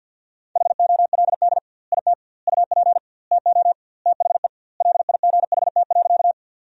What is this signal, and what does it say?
Morse code, heard as a keyed tone, about 50 wpm, 700 Hz. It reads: HOLD IT UP TO THE LIGHT.
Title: 2x Word-Spacing - Sentences from Top 400 Words - 50wpm